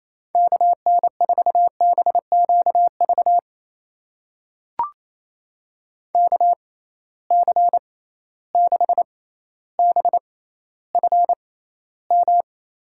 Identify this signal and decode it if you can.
Morse code, heard as a keyed tone, about 28 wpm, spaced as for 14 wpm, 700 Hz. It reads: KN4BQV KC6BFM
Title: USA Call Signs - 6 Characters - 6x Character Spacing - 28wpm